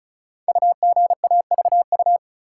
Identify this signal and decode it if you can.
Morse code, heard as a keyed tone, about 36 wpm, 700 Hz. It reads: UGAVU
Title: Code Groups - 5 Letter:Numbers - 35wpm